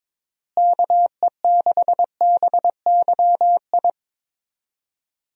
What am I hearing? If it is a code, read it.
Morse code, heard as a keyed tone, about 22 wpm, 700 Hz. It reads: KE6BYI
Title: USA Call Signs - 6 Characters - Repeated 3x - 22wpm